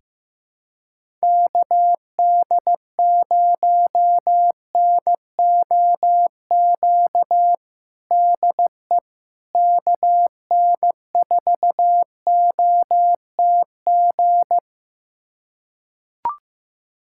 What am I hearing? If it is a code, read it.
Morse code, heard as a keyed tone, about 15 wpm, 700 Hz. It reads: KD0NOQ DE KN4OTG
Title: USA Callsign DE Callsign - 6 Characters - 15wpm